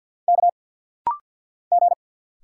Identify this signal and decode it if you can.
Morse code, heard as a keyed tone, about 50 wpm, 700 Hz. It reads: K G
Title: Single Letters - Rapid-Fire 50wpm